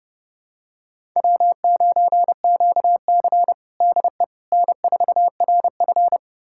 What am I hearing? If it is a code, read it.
Morse code, heard as a keyed tone, about 32 wpm, 700 Hz. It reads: W9QC DE N4RF
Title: USA Callsign DE Callsign - 4 Characters - 30wpm